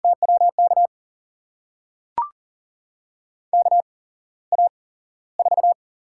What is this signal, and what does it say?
Morse code, heard as a keyed tone, about 42 wpm, spaced as for 17 wpm, 700 Hz. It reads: TWK KA4
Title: USA Call Signs - 8x Character Spacing - 40wpm